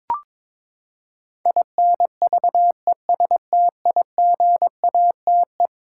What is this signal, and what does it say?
Morse code, heard as a keyed tone, about 22 wpm, 700 Hz. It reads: INVESTIGATE
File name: practice / 22wpm / Eleven Letter Words - 22wpm